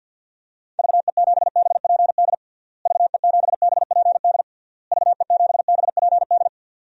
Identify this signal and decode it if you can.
Morse code, heard as a keyed tone, about 52 wpm, 700 Hz. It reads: VE7BPD VE7BPD VE7BPD
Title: DXCC-100 Call Signs - 6 Characters - Repeated 3x - 50wpm